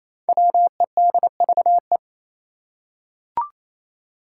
Morse code, heard as a keyed tone, about 28 wpm, 700 Hz. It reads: WEDVE
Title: Five Challenging Letter-Number Combinations 28wpm